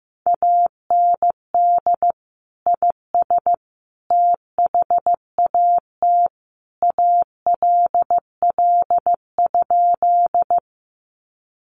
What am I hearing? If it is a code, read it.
Morse code, heard as a keyed tone, about 15 wpm, 700 Hz. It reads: AND IS THAT ALL?
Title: Sentences from Top 100 Words 15wpm